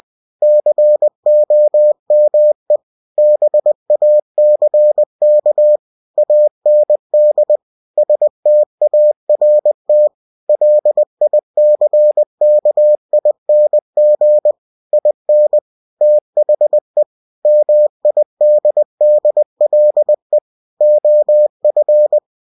Morse code, heard as a keyed tone, about 20 wpm, 600 Hz. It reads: COME BACK AND START LICKING IN THE MIDDLE OF
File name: cw-books / winnie-the-pooh / winnie-20wpm-page-0036